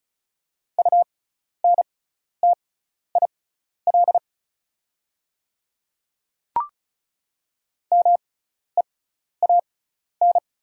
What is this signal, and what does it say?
Morse code, heard as a keyed tone, about 36 wpm, spaced as for 12 wpm, 700 Hz. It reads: UNTIL MEAN
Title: 6x Character Spacing - Top 300 Words - 35wpm